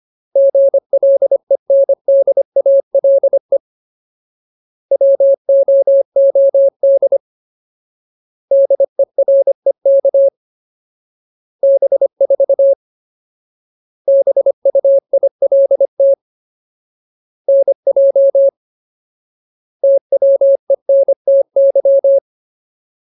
Morse code, heard as a keyed tone, about 25 wpm, 550 Hz. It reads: GLENDALE WOOD DEREK B4 BUILT NJ TWENTY
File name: Ultimate Word List / 25WPM Ultimate